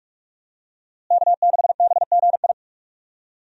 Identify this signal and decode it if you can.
Morse code, heard as a keyed tone, about 41 wpm, 700 Hz. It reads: K6BGI
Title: USA Call Signs - 5 Characters - Repeated 2x - 45wpm